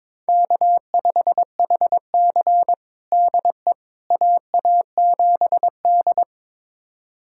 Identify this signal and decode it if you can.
Morse code, heard as a keyed tone, about 22 wpm, 700 Hz. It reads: K5HC DE AA7D
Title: USA Callsign DE Callsign - 4 Characters - 22wpm